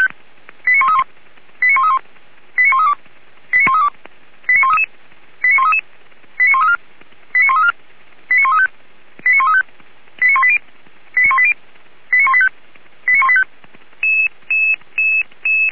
Five-tone selective call (ZVEI), heard as quick runs of five tones, tones of about 70 ms. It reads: 89131 89131 89132 89132 89133 89133 89134 89134 89135 89135 89180 89180 89185 89185